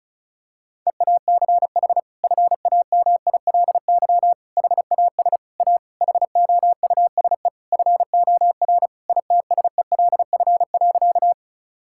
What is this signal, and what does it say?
Morse code, heard as a keyed tone, about 34 wpm, 700 Hz. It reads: EACH FAMILY HAS A HOUSE FOR ITSELF.
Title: Sentences from Top 1000 Words - Review Part 5 - 35wpm